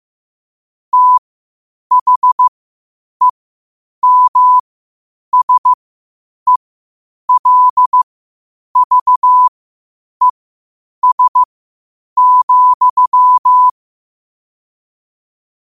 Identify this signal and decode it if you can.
Morse code, heard as a keyed tone, about 15 wpm, spaced as for 8 wpm, 1000 Hz. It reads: THEMSELVES,
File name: qotdfiles / 8wpm